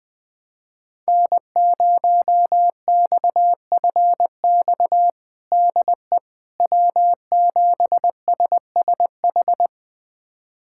Morse code, heard as a keyed tone, about 20 wpm, 700 Hz. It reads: N0XFX DE W7SSH
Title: USA Callsign DE Callsign - 5 Characters - 20wpm